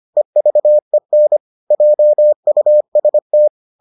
Morse code, heard as a keyed tone, about 25 wpm, 600 Hz. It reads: EVEN JUST